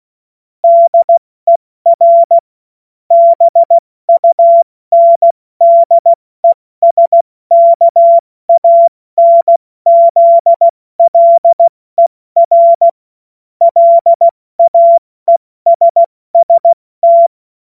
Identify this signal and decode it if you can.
Morse code, heard as a keyed tone, about 16 wpm, 680 Hz. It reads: DER BUNDESKANZLER LAESST
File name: Morsekurs / Witze / Witz 03 Juni Mai 2025 70 F80